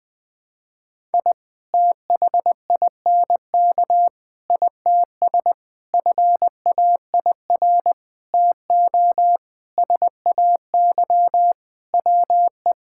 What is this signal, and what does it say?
Morse code, heard as a keyed tone, about 20 wpm, 700 Hz. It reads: I THINK ITS FAIR TO SAY WE
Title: Sentences with 6-Word Phrases - Spoken Beginning - 20wpm